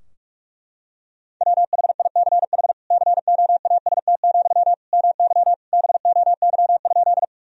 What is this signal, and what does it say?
Morse code, heard as a keyed tone, about 46 wpm, 700 Hz. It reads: WHICH COAST, MY BOY?
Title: Sentences from Top 1000 Words - Review Part 4 - 45wpm